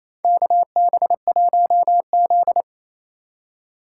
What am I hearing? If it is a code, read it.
Morse code, heard as a keyed tone, about 28 wpm, 700 Hz. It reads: KB1Z
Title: USA Call Signs - 3x Character Spacing - 28wpm